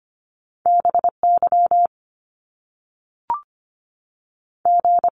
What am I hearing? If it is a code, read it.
Morse code, heard as a keyed tone, about 25 wpm, 700 Hz. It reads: B Y G
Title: Top Two Letter Combinations in Words 25wpm